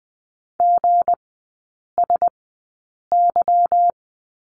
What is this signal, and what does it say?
Morse code, heard as a keyed tone, about 20 wpm, spaced as for 8 wpm, 690 Hz. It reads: GSY
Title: 159 Lesson 26e - Letter g - Three Characters 20wpm